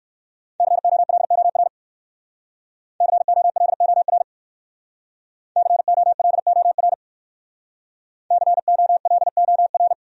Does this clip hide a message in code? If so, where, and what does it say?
Yes: Morse code from 0.6 to 9.9 s, heard as a keyed tone, about 65 wpm, 700 Hz. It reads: COLOR COLOR COLOR COLOR